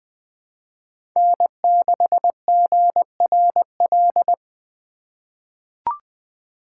Morse code, heard as a keyed tone, about 20 wpm, 700 Hz. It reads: N6GRL
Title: USA Call Signs - 5 Characters - 3x Character Spacing - 20wpm